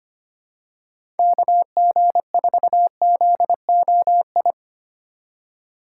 Morse code, heard as a keyed tone, about 25 wpm, 700 Hz. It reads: KG4ZOI
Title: USA Call Signs - 6 Characters - 3x Character Spacing - 25wpm